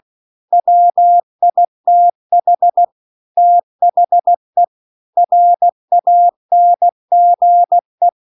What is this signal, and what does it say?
Morse code, heard as a keyed tone, about 16 wpm, 700 Hz. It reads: WITH THE RANGE